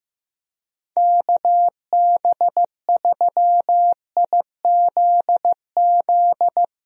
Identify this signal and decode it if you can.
Morse code, heard as a keyed tone, about 15 wpm, 700 Hz. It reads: KB3IZZ
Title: USA Call Signs - 6 Characters - Repeated 2x - 15wpm